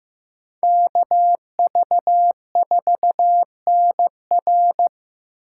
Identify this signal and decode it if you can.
Morse code, heard as a keyed tone, about 18 wpm, 700 Hz. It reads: KV4NR
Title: Speed Racing - USA 2x2 Call Signs 15wpm